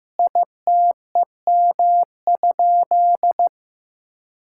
Morse code, heard as a keyed tone, about 15 wpm, 700 Hz. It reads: ITEM?
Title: Sentences from Top 1000 Words - Review Part 6 - 15wpm